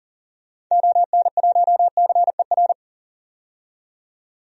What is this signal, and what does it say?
Morse code, heard as a keyed tone, about 38 wpm, 700 Hz. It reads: ON1CER
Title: DXCC-100 Call Signs - 6 Characters - 40wpm